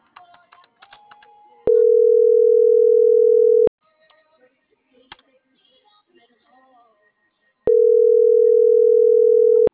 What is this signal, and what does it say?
Ringback tone (North America), 440 + 480 Hz.